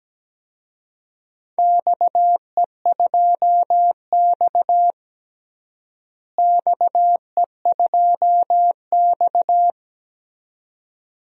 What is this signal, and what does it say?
Morse code, heard as a keyed tone, about 17 wpm, 700 Hz. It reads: XE2X XE2X
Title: DXCC-100 Call Signs - 4 Characters - Repeated 2x - 17wpm